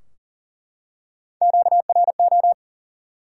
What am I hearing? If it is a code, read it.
Morse code, heard as a keyed tone, about 40 wpm, 700 Hz. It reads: QRO
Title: Top 90 QSO Elements 40wpm